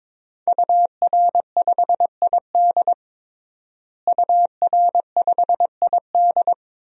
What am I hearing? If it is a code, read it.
Morse code, heard as a keyed tone, about 22 wpm, 700 Hz. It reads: UR5ID UR5ID
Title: DXCC-100 Call Signs - 5 Characters - Repeated 3x - 22wpm